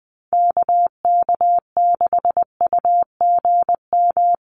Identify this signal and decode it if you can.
Morse code, heard as a keyed tone, about 20 wpm, 700 Hz. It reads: KK6UGM
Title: USA Call Signs 20wpm